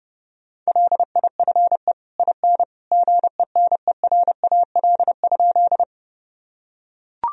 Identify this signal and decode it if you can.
Morse code, heard as a keyed tone, about 30 wpm, 700 Hz. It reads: LIFE IN GENERAL?
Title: Sentences from Top 1000 Words - Review Part 3 - 30wpm